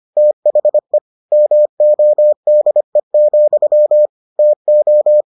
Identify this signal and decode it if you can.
Morse code, heard as a keyed tone, about 25 wpm, 600 Hz. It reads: THE MODE, TO